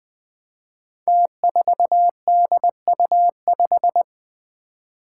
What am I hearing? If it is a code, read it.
Morse code, heard as a keyed tone, about 20 wpm, 700 Hz. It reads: T4DU5
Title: Five Challenging Letter-Number Combinations 20wpm